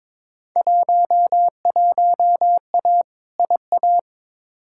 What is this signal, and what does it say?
Morse code, heard as a keyed tone, about 22 wpm, 700 Hz. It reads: 11A IA
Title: Field Day Exchanges Encore - 22wpm